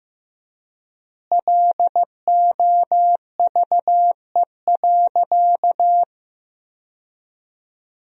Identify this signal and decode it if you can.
Morse code, heard as a keyed tone, about 15 wpm, 700 Hz. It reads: LOVE.
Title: 3x Word-Spacing - Sentences from Top 400 Words - 15wpm